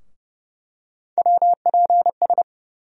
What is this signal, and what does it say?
Morse code, heard as a keyed tone, about 30 wpm, 700 Hz. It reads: WPS
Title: USA Call Sign Postfixes - 3 Letters 30wpm